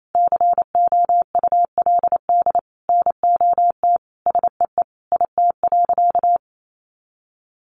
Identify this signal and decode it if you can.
Morse code, heard as a keyed tone, about 28 wpm, 700 Hz. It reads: COULD NOT SEE IT.
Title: Sentences from Top 1000 Words 28wpm